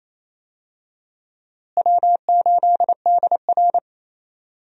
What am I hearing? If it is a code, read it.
Morse code, heard as a keyed tone, about 28 wpm, 700 Hz. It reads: W8DR
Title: USA Call Signs - 4 Characters - Repeated 2x - 28wpm